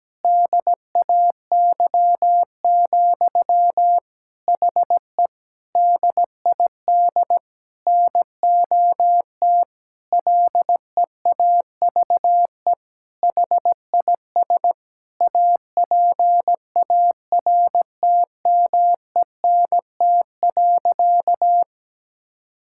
Morse code, heard as a keyed tone, about 17 wpm, 700 Hz. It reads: DAY, HE DID NOT LEAVE HIS APARTMENT.
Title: Sentences from Top 1000 Words - Review Part 6 - 17wpm